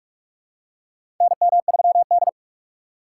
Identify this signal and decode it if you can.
Morse code, heard as a keyed tone, about 41 wpm, 700 Hz. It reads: NM3D
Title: USA Call Signs - 4 Characters - 2x Character Spacing - 45wpm